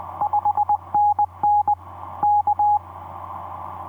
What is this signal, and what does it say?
Morse code, heard as a keyed tone, about 20 wpm, 830 Hz. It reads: 5NN K
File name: v5-dk1ce dm1tde etc 17m cw 28dec2010 0942